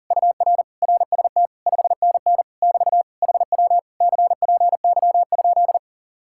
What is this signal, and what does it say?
Morse code, heard as a keyed tone, about 51 wpm, 700 Hz. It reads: UR RST 5NN <BT> HW CPY?